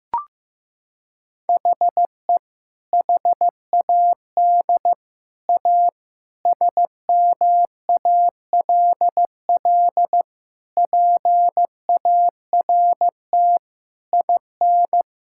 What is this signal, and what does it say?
Morse code, heard as a keyed tone, about 16 wpm, 700 Hz. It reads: HE HAD A SMALL PART IN
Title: Sentences from Top 1000 Words - Review Part 5 - 15wpm